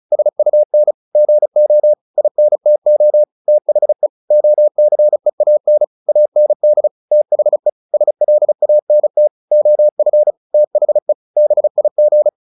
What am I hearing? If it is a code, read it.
Morse code, heard as a keyed tone, about 34 wpm, 600 Hz. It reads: SUN GO INTO THE OCEAN AND THE SLANT OF THE BIG
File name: the old man and the sea 2 35wpm